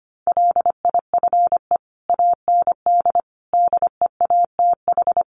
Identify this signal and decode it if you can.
Morse code, heard as a keyed tone, about 25 wpm, 700 Hz. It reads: LIFE AND DEATH
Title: Sentences from Top 1100 Words 25wpm